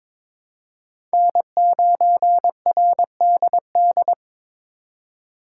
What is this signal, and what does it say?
Morse code, heard as a keyed tone, about 22 wpm, 700 Hz. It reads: N9RDD
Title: USA Call Signs - 5 Characters - 2x Character Spacing - 22wpm